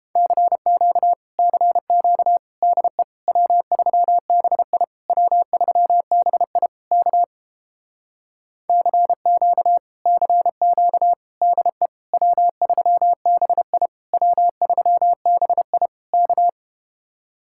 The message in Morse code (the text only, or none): CQ CQ DE W3BI W3BI K CQ CQ DE W3BI W3BI K